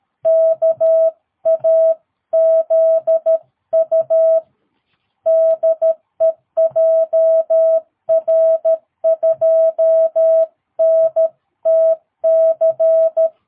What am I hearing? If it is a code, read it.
Morse code, heard as a keyed tone, about 13 wpm, 650 Hz. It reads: KAZU DE JR2NTC
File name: jh8cbh / 2022cw